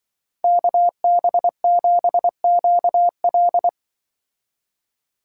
Morse code, heard as a keyed tone, about 23 wpm, 700 Hz. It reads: KB7QL
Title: Speed Racing - USA 2x2 Call Signs 20wpm